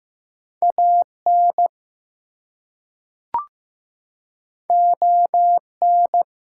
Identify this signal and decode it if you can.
Morse code, heard as a keyed tone, about 15 wpm, 700 Hz. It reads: AN ON
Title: Two Letter Words - 15wpm